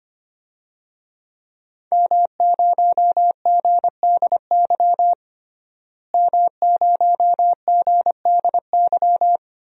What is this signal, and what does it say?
Morse code, heard as a keyed tone, about 25 wpm, 700 Hz. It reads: M0GDY M0GDY
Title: DXCC-100 Call Signs - 5 Characters - Repeated 2x - 25wpm